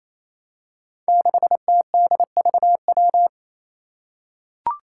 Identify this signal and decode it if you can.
Morse code, heard as a keyed tone, about 28 wpm, 700 Hz. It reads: 6TDVW
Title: Five Challenging Letter-Number Combinations 28wpm